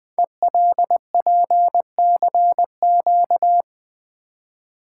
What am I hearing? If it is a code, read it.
Morse code, heard as a keyed tone, about 20 wpm, 700 Hz. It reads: ELPCQ
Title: Code Groups - 5 Letters - 20wpm